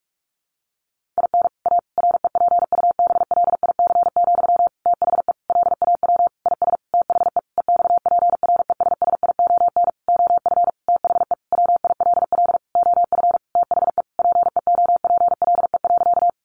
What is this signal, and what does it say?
Morse code, heard as a keyed tone, about 44 wpm, 700 Hz. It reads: IN A REPUBLIC, THE LAW IS THE EXPRESSION OF THE WILL OF THE PEOPLE.
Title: Sentences from Top 1500 Words 45wpm